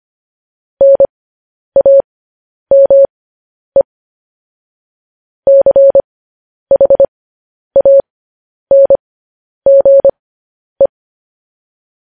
Morse code, heard as a keyed tone, about 25 wpm, spaced as for 10 wpm, 550 Hz. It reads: NAME CHANGE
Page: news-headlines-in-morse-code-at-10-wpm